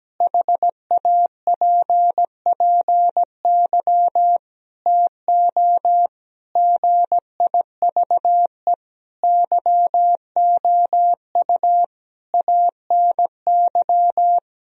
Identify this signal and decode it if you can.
Morse code, heard as a keyed tone, about 17 wpm, 700 Hz. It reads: HAPPY TO GIVE YOU ANY
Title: Sentences from Top 1000 Words - Review Part 3 - 17wpm